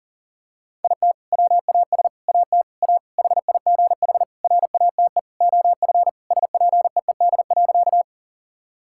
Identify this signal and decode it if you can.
Morse code, heard as a keyed tone, about 37 wpm, 700 Hz. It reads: IT WAS AT A HIGH RATE OF SPEED.
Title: Sentences from Top 1000 Words - Review Part 5 - 40wpm